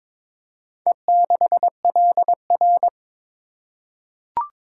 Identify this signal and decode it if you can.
Morse code, heard as a keyed tone, about 22 wpm, 700 Hz. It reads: E6LR
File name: practice / 22wpm / Four Letter-Number Combinations - 22wpm